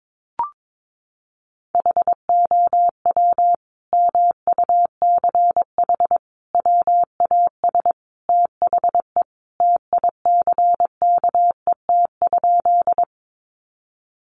Morse code, heard as a keyed tone, about 22 wpm, 700 Hz. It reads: HOW MUCH WAS THE TICKET?
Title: Sentences from Top 1400 Words 22wpm